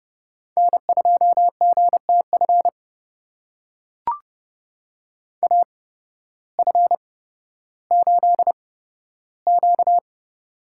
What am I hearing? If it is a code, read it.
Morse code, heard as a keyed tone, about 30 wpm, spaced as for 14 wpm, 700 Hz. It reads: N2GTF AF8Q
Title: USA Call Signs - 8x Character Spacing - 30wpm